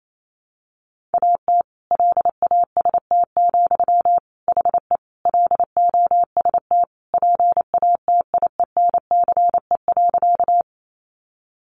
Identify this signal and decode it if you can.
Morse code, heard as a keyed tone, about 28 wpm, 700 Hz. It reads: AT LAST, HE LOST PATIENCE.
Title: Sentences from Top 1400 Words 28wpm